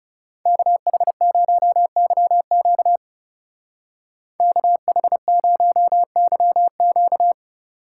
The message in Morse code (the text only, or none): KH0YQ KH0YQ